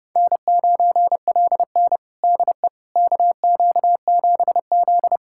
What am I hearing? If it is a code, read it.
Morse code, heard as a keyed tone, about 30 wpm, 700 Hz. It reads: N9LN DE KQ7Z
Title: USA Callsign DE Callsign - 4 Characters - 30wpm